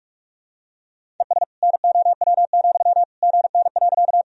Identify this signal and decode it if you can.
Morse code, heard as a keyed tone, about 42 wpm, 700 Hz. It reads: ES NOW, GN.
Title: Sentences from Top 300 Words and Abbreviations 45wpm